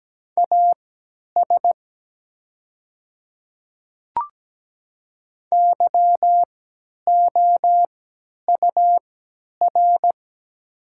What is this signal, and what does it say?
Morse code, heard as a keyed tone, about 17 wpm, spaced as for 10 wpm, 700 Hz. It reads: AS YOUR
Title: 3x Character Spacing - Top 100 Words - 17wpm